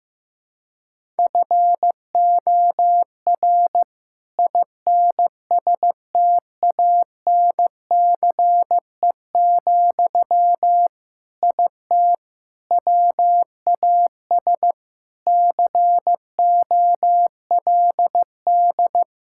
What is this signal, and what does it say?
Morse code, heard as a keyed tone, about 16 wpm, 700 Hz. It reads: FOR INSTANCE, IT WAS COLD
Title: Sentences from Top 1000 Words - Review Part 5 - 15wpm